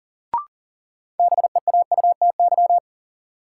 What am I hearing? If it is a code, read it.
Morse code, heard as a keyed tone, about 39 wpm, 700 Hz. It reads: BEAUTY
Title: Sentences from Top 600 Words - New Word Drill - 40wpm